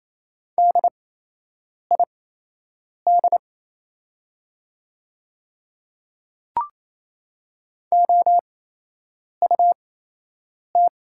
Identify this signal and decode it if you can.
Morse code, heard as a keyed tone, about 28 wpm, spaced as for 8 wpm, 700 Hz. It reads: DID OUT
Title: 8x Character Spacing - Top 100 Words - 28wpm